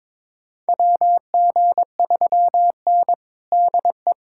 Morse code, heard as a keyed tone, about 21 wpm, 700 Hz. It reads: WG3N DE
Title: USA Callsign DE Callsign - 4 Characters - 22wpm